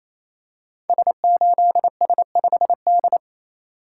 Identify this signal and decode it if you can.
Morse code, heard as a keyed tone, about 27 wpm, 700 Hz. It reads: S8S5D
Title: Code Groups - 5 Letter:Numbers - 28wpm